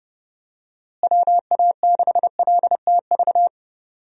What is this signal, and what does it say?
Morse code, heard as a keyed tone, about 30 wpm, 700 Hz. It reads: WA6LTV
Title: USA Call Signs - 6 Characters - Repeated 3x - 30wpm